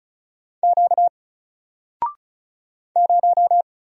Morse code, heard as a keyed tone, about 36 wpm, 700 Hz. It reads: Q 0
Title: Single Character (Letter:Number:Punctuation:Prosign) 35wpm